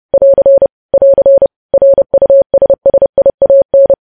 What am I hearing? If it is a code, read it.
Morse code, heard as a keyed tone, about 30 wpm, 550 Hz. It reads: <AR> <AR> RUSSIAN
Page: news headlines in morse code at 30 wpm